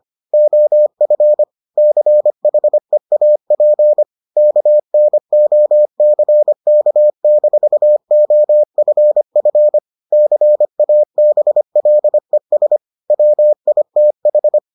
Morse code, heard as a keyed tone, about 25 wpm, 600 Hz. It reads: OF CHEAP KNOCK-OFF CABLES WITH